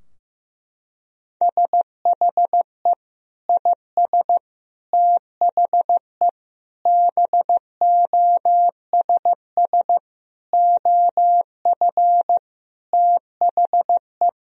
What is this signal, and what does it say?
Morse code, heard as a keyed tone, about 16 wpm, 700 Hz. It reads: SHE IS THE BOSS OF THE
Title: Sentences from Top 1000 Words - Review Part 4 - 15wpm